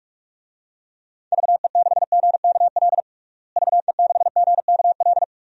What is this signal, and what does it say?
Morse code, heard as a keyed tone, about 46 wpm, 700 Hz. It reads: VE6GKL VE6GKL
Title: DXCC-100 Call Signs - 6 Characters - Repeated 2x - 45wpm